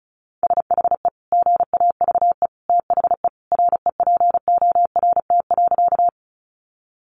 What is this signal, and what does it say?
Morse code, heard as a keyed tone, about 34 wpm, 700 Hz. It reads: SHE GAVE THE REPORT.
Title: Sentences from Top 900 Words 35wpm